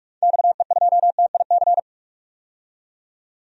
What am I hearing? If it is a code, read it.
Morse code, heard as a keyed tone, about 41 wpm, 700 Hz. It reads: XE2TIC
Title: DXCC-100 Call Signs - 6 Characters - 45wpm